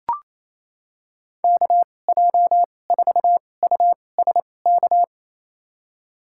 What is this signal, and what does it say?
Morse code, heard as a keyed tone, about 28 wpm, spaced as for 20 wpm, 700 Hz. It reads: KJ4USK
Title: USA Call Signs - 6 Characters - 2x Character Spacing - 28wpm